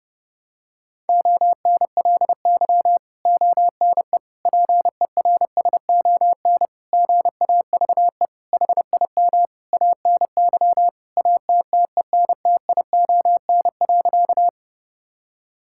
Morse code, heard as a keyed tone, about 30 wpm, 700 Hz. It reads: ONLY ONE PERSON GAVE HIM ANY ATTENTION.
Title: Sentences from Top 1000 Words - Review Part 6 - 30wpm